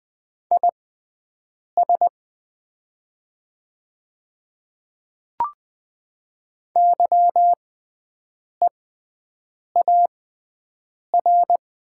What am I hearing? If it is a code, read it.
Morse code, heard as a keyed tone, about 20 wpm, spaced as for 7 wpm, 700 Hz. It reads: IS YEAR
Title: 6x Character Spacing - Top 300 Words - 20wpm